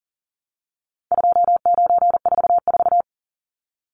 Morse code, heard as a keyed tone, about 40 wpm, 700 Hz. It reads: 2944
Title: Numbers - 4-Digit 40wpm